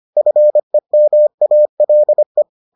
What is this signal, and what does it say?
Morse code, heard as a keyed tone, about 25 wpm, 600 Hz. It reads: FEMALE